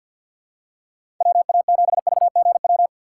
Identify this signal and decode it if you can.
Morse code, heard as a keyed tone, about 47 wpm, 700 Hz. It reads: WA7VGW
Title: USA Call Signs - 6 Characters - 6x Character Spacing - 50wpm